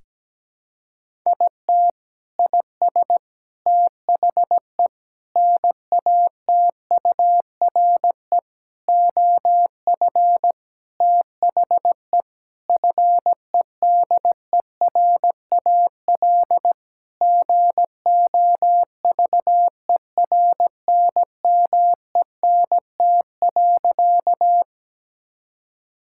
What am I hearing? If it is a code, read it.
Morse code, heard as a keyed tone, about 17 wpm, 700 Hz. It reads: IT IS THE NATURE OF THE FEDERAL GOVERNMENT.
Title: Sentences from Top 1000 Words - Review Part 5 - 17wpm